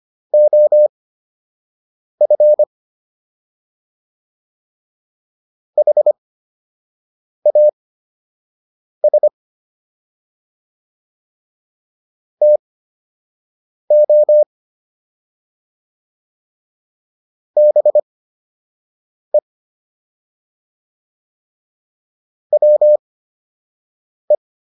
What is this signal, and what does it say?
Morse code, heard as a keyed tone, about 25 wpm, spaced as for 6 wpm, 600 Hz. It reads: OF HAS TO BE WE